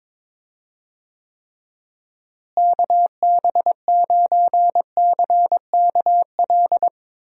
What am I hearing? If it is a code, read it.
Morse code, heard as a keyed tone, about 22 wpm, 700 Hz. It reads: KB9CKL
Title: USA Call Signs - 6 Characters - Repeated 2x - 22wpm